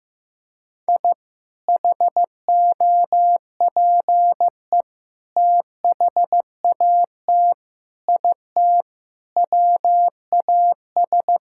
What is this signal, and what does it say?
Morse code, heard as a keyed tone, about 15 wpm, 700 Hz. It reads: I HOPE THAT IT WAS
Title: Sentences from Top 1000 Words - Review Part 5 - 15wpm